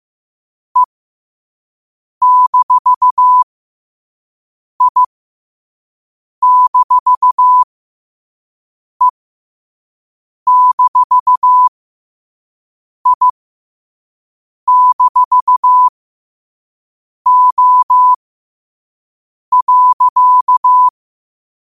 Morse code, heard as a keyed tone, about 15 wpm, spaced as for 5 wpm, 1000 Hz. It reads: E-I-E-I-O.
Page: New quotes every day in morse code at 5 Words per minute.